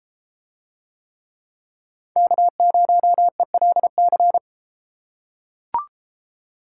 Morse code, heard as a keyed tone, about 32 wpm, 700 Hz. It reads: K0ELC